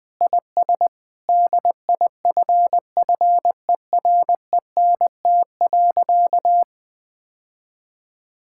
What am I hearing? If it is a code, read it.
Morse code, heard as a keyed tone, about 21 wpm, 700 Hz. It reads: IS DIFFERENT.
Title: Sentences with 8-Word Phrases - 20wpm